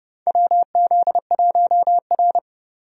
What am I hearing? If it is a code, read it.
Morse code, heard as a keyed tone, about 29 wpm, 700 Hz. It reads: WZ1R
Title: USA Call Signs - 4 Characters - 30wpm